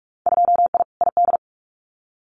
Morse code, heard as a keyed tone, about 46 wpm, 690 Hz. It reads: 2I ID
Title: Winter Field Day Exchanges Encore 45wpm